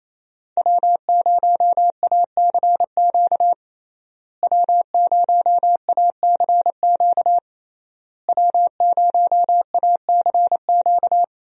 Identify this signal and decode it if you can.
Morse code, heard as a keyed tone, about 28 wpm, 700 Hz. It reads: W0ACQ W0ACQ W0ACQ
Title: USA Call Signs - 5 Characters - Repeated 3x - 28wpm